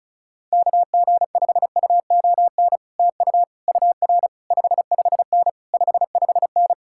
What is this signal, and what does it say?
Morse code, heard as a keyed tone, about 38 wpm, 700 Hz. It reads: KG5UON TU UR 55N 55N